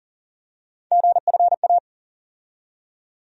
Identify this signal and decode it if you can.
Morse code, heard as a keyed tone, about 40 wpm, 700 Hz. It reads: GFA
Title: USA Call Sign Postfixes - 3 Letters 40wpm